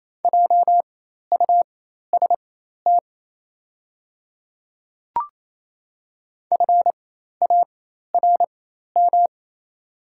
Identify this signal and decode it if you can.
Morse code, heard as a keyed tone, about 28 wpm, spaced as for 13 wpm, 700 Hz. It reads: JUST FARM
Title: 4x Character Spacing - Top 500 Words - 28wpm